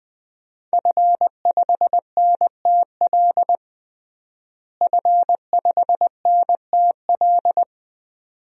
Morse code, heard as a keyed tone, about 20 wpm, 700 Hz. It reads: F5NTL F5NTL
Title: DXCC-100 Call Signs - 5 Characters - Repeated 2x - 20wpm